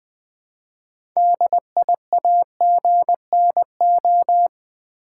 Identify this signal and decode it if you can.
Morse code, heard as a keyed tone, about 20 wpm, 700 Hz. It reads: DIAGNO
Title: Build A Word - 3 Letters At A Time - 9-Letter Words - 20wpm